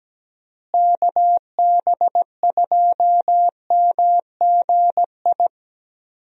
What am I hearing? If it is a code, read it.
Morse code, heard as a keyed tone, about 17 wpm, 700 Hz. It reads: KB2MGI
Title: USA Call Signs - 6 Characters - Repeated 2x - 17wpm